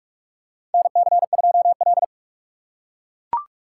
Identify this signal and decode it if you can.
Morse code, heard as a keyed tone, about 42 wpm, 700 Hz. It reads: NC2L
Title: USA Call Signs - 4 Characters - 6x Character Spacing - 45wpm